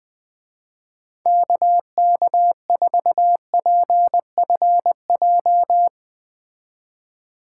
Morse code, heard as a keyed tone, about 21 wpm, 700 Hz. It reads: KK4PFJ